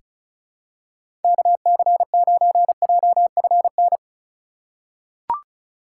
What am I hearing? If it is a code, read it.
Morse code, heard as a keyed tone, about 34 wpm, 700 Hz. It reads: KC9JFN
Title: USA Call Signs - 2x Character Spacing - 35wpm